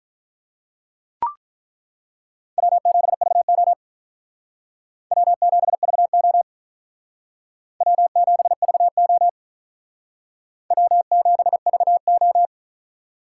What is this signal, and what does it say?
Morse code, heard as a keyed tone, about 42 wpm, 700 Hz. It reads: W7VO W7VO W7VO W7VO
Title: Speed Racing - USA 1x2 Call Signs 35wpm